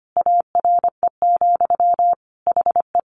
Morse code, heard as a keyed tone, about 25 wpm, 700 Hz. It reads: ARE, HE
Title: Sentences from Top 1100 Words 25wpm